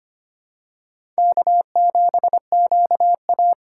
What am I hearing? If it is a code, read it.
Morse code, heard as a keyed tone, about 25 wpm, 700 Hz. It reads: K7QA
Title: USA Call Signs - 4 Characters - 3x Character Spacing - 25wpm